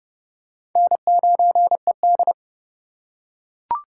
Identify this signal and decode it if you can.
Morse code, heard as a keyed tone, about 30 wpm, 700 Hz. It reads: N9ED
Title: USA Call Signs - 4x Character Spacing - 30wpm